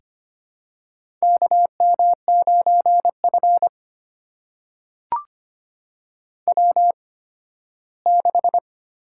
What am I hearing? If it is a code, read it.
Morse code, heard as a keyed tone, about 25 wpm, spaced as for 15 wpm, 700 Hz. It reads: KM9F W6
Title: USA Call Signs - 4 Characters - 8x Character Spacing - 25wpm